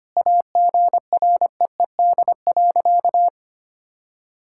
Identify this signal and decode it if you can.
Morse code, heard as a keyed tone, about 25 wpm, 700 Hz. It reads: AGREED.
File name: practice / 25wpm / Sentences with 4-Word Phrases - 25wpm